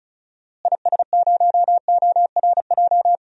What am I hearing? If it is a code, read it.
Morse code, heard as a keyed tone, about 36 wpm, 700 Hz. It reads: IS0ORJ
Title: DXCC-100 Call Signs - 6 Characters - Repeated 2x - 35wpm